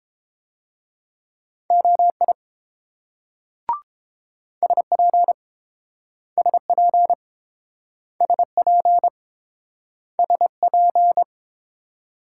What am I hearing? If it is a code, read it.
Morse code, heard as a keyed tone, about 28 wpm, 700 Hz. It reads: OI SP SP SP SP